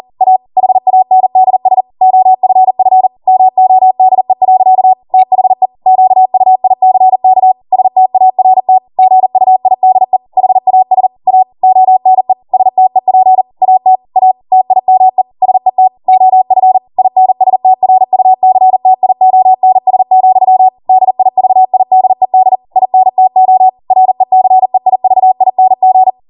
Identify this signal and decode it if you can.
Morse code, heard as a keyed tone, about 40 wpm, 750 Hz. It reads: A HANDS OFF MODE. THE QUICK START GUIDE HAS A ONE STEP AT A TIME SET OF INSTRUCTIONS, DIVIDED INTO RECEIVING